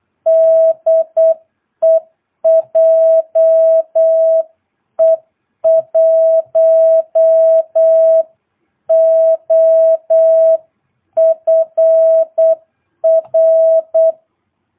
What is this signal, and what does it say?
Morse code, heard as a keyed tone, about 8 wpm, 650 Hz. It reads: DE JE1OFR
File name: jh8cbh / 2022cw